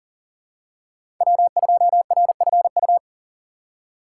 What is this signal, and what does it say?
Morse code, heard as a keyed tone, about 40 wpm, 700 Hz. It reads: W2RFU
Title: USA Call Signs - 5 Characters - 3x Character Spacing - 40wpm